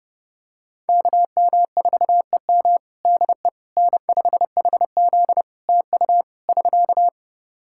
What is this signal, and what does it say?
Morse code, heard as a keyed tone, about 32 wpm, 700 Hz. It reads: KM4EM DE N5HZ TU <SK>
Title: Practice QSO - Exchange 4 - Ending - 30wpm